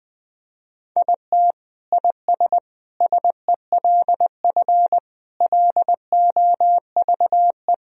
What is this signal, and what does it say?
Morse code, heard as a keyed tone, about 19 wpm, 700 Hz. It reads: IT IS SELF LOVE
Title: Sentences from Top 1000 Words - Review Part 1 - 20wpm